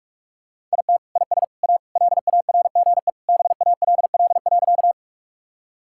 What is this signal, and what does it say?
Morse code, heard as a keyed tone, about 46 wpm, 690 Hz. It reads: IT IS A LARGE BALL.
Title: Sentences from Top 1000 Words - Review Part 6 - 45wpm